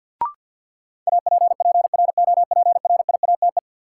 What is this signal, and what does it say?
Morse code, heard as a keyed tone, about 46 wpm, 700 Hz. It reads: APPROPRIATE
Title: Sentences from Top 900 Words - New Word Drill - 50wpm